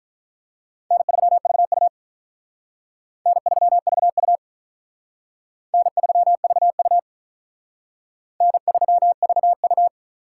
Morse code, heard as a keyed tone, about 44 wpm, 700 Hz. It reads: N3VU N3VU N3VU N3VU